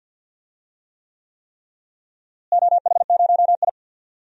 Morse code, heard as a keyed tone, about 50 wpm, 700 Hz. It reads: OH0I
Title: DXCC-100 Call Signs - 4 Characters - Repeated 3x - 50wpm